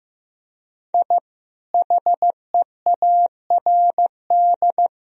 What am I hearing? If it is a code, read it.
Morse code, heard as a keyed tone, about 15 wpm, 700 Hz. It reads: I HEARD
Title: Sentences from Top 1000 Words - Review Part 2 - 15wpm